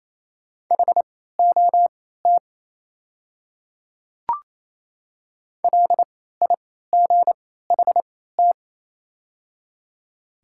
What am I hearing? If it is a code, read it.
Morse code, heard as a keyed tone, about 28 wpm, spaced as for 16 wpm, 700 Hz. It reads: HOT LIGHT
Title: 3x Character Spacing - Top 200 Words - 28wpm